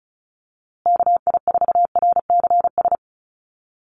Morse code, heard as a keyed tone, about 34 wpm, 700 Hz. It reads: KI4RCS
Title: USA Call Signs Repeated 2x 35wpm